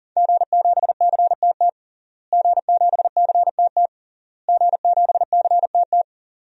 Morse code, heard as a keyed tone, about 40 wpm, 700 Hz. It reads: G7CTT G7CTT G7CTT
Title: DXCC-100 Call Signs - 5 Characters - Repeated 3x - 40wpm